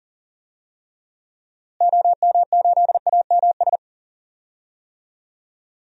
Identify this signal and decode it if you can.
Morse code, heard as a keyed tone, about 40 wpm, 700 Hz. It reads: OM8AMS
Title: DXCC-100 Call Signs - 6 Characters - Repeated 3x - 40wpm